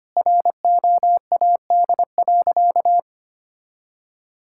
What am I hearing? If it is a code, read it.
Morse code, heard as a keyed tone, about 25 wpm, 700 Hz. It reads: ROAD.
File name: practice / 25wpm / Sentences with 4-Word Phrases - 25wpm